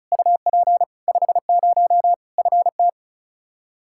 Morse code, heard as a keyed tone, about 34 wpm, 700 Hz. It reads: UP 50 FT
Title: K1USN SST QSO Practice 35wpm